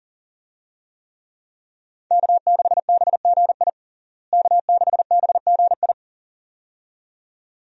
Morse code, heard as a keyed tone, about 37 wpm, 700 Hz. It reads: K6BGI K6BGI
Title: USA Call Signs - 5 Characters - Repeated 2x - 40wpm